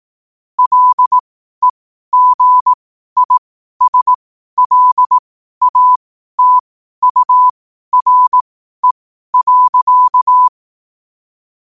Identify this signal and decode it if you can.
Morse code, heard as a keyed tone, about 18 wpm, spaced as for 12 wpm, 1000 Hz. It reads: LEGISLATURE.
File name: qotdfiles / 12wpm